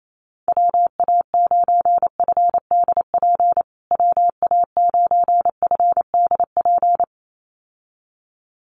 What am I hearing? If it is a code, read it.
Morse code, heard as a keyed tone, about 28 wpm, 700 Hz. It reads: WA9FDP WA9FDP
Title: USA Call Signs Repeated 2x 28wpm